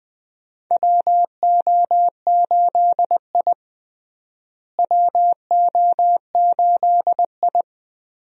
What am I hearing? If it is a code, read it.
Morse code, heard as a keyed tone, about 20 wpm, 700 Hz. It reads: WO8I WO8I
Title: USA Call Signs - 4 Characters - Repeated 3x - 20wpm